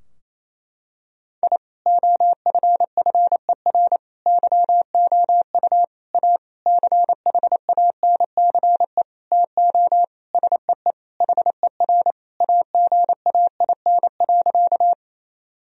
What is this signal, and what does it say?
Morse code, heard as a keyed tone, about 28 wpm, 700 Hz. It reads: I OFFER YOU A CHANCE TO SEE HER AGAIN.
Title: Sentences from Top 1000 Words - Review Part 4 - 28wpm